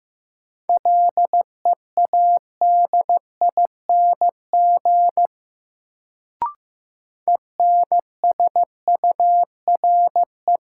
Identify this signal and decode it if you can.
Morse code, heard as a keyed tone, about 15 wpm, 700 Hz. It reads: LEADING ENSURE
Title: Sentences from Top 1100 Words - New Word Drill - 15wpm